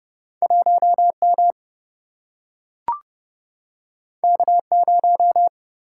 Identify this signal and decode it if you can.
Morse code, heard as a keyed tone, about 30 wpm, 700 Hz. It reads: 1M K0
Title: Single-Letter Single-Number 30wpm